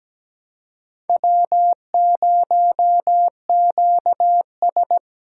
Morse code, heard as a keyed tone, about 20 wpm, 700 Hz. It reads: W0QS
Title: Speed Racing - USA 1x2 Call Signs 17wpm